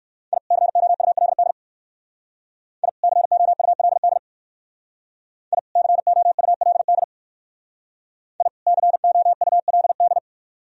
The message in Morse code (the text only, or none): I COULD I COULD I COULD I COULD